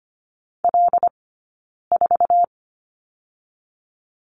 Morse code, heard as a keyed tone, about 25 wpm, 700 Hz. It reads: L 4
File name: course / 25:10wpm / 092 Lesson 16d - Letter c - Two Characters 25wpm